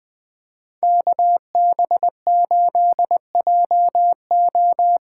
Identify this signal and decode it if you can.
Morse code, heard as a keyed tone, about 20 wpm, 700 Hz. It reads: KB8JO
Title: USA Call Signs - 6 Characters - 6x Character Spacing - 20wpm